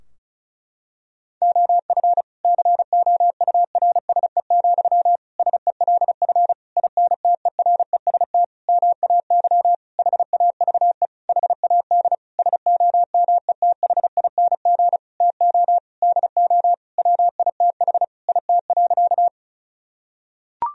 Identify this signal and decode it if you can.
Morse code, heard as a keyed tone, about 34 wpm, 700 Hz. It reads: OF COURSE, SELF INTEREST MAY HAVE HAD SOMETHING TO DO WITH IT.
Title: Sentences from Top 1000 Words - Review Part 2 - 35wpm